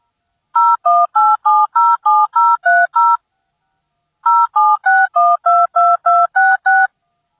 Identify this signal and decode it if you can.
DTMF, heard as telephone touch tones, tones of about 220 ms. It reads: 018707030 076122266